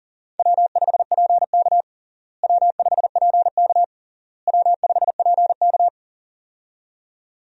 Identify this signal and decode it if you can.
Morse code, heard as a keyed tone, about 40 wpm, 700 Hz. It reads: W5PK W5PK W5PK
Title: USA Call Signs - 4 Characters - Repeated 3x - 40wpm